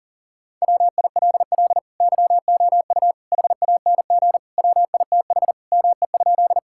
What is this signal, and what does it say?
Morse code, heard as a keyed tone, about 40 wpm, 700 Hz. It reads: WILL YOU HANG WITH ME?
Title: Sentences from Top 1600 Words 40wpm